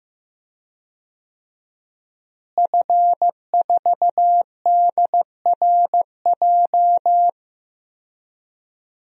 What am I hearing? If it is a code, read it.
Morse code, heard as a keyed tone, about 15 wpm, 700 Hz. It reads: F4DRJ
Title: DXCC-100 Call Signs - 5 Characters - Repeated 3x - 15wpm